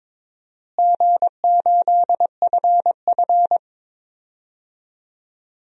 Morse code, heard as a keyed tone, about 22 wpm, 700 Hz. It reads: G8FF
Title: DXCC-100 Call Signs - 4 Characters - Repeated 2x - 22wpm